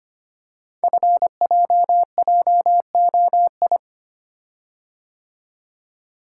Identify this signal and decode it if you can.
Morse code, heard as a keyed tone, about 25 wpm, 700 Hz. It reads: FJJOI
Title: Code Groups - 5 Letters - 25wpm